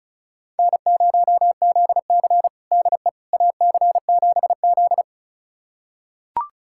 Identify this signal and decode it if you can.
Morse code, heard as a keyed tone, about 34 wpm, 700 Hz. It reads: N0ZC DE AC7Z
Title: USA Callsign DE Callsign - 4 Characters - 35wpm